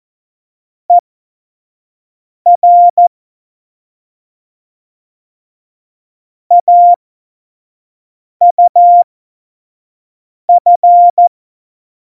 Morse code, heard as a keyed tone, about 14 wpm, spaced as for 5 wpm, 700 Hz. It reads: ER AUF
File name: Morsekurs / Witze / Witz 24. November 2020, 25 F70